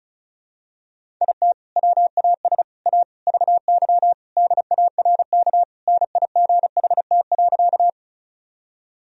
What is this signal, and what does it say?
Morse code, heard as a keyed tone, about 34 wpm, 700 Hz. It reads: IT WAS A VY DARK NIGHT.
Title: Sentences from Top 500 Words and Abbreviations 35wpm